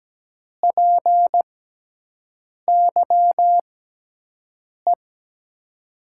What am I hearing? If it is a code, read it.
Morse code, heard as a keyed tone, about 17 wpm, spaced as for 6 wpm, 700 Hz. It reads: PYE
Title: USA Call Signs - 6 Characters - 6x Character Spacing - 17wpm